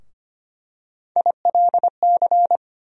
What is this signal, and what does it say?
Morse code, heard as a keyed tone, about 25 wpm, 700 Hz. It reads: ILC
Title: USA Call Sign Postfixes - 3 Letters 25wpm